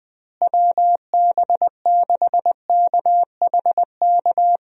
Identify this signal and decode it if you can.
Morse code, heard as a keyed tone, about 21 wpm, 700 Hz. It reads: WB6KHK
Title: USA Call Signs - 6 Characters - Repeated 3x - 20wpm